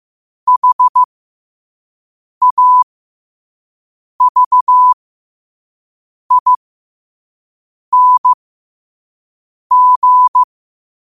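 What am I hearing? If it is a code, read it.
Morse code, heard as a keyed tone, about 15 wpm, spaced as for 5 wpm, 1000 Hz. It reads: HAVING
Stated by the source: Quote-of-the-day Morse code practice at 5 wpm.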